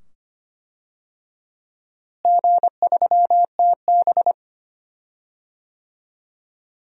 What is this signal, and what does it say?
Morse code, heard as a keyed tone, about 25 wpm, 700 Hz. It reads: G3TB
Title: DXCC-100 Call Signs - 4 Characters - Repeated 2x - 25wpm